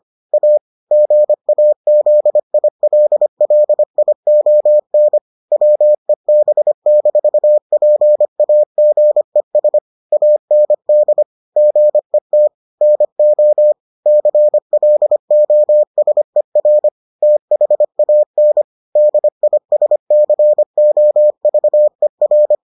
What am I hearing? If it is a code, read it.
Morse code, heard as a keyed tone, about 25 wpm, 600 Hz. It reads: A GAZILLION WEB-PAGES AND GET NO CLOSER THAN DISCOVER